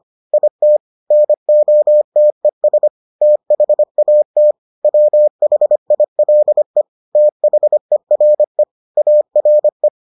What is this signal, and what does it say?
Morse code, heard as a keyed tone, about 25 wpm, 600 Hz. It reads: IT NOTES THAT WHILE THERE ARE